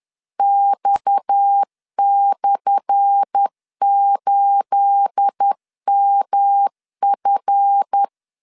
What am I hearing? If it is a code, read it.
Morse code, heard as a keyed tone, about 10 wpm, 800 Hz. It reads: X/8MF
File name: LEZIONI / 10 SIGNS / 10A SLASH 10wpm 2808